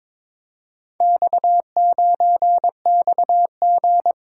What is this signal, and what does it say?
Morse code, heard as a keyed tone, about 23 wpm, 700 Hz. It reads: X9XG
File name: practice / 22wpm / Four Letter-Number Combinations - 22wpm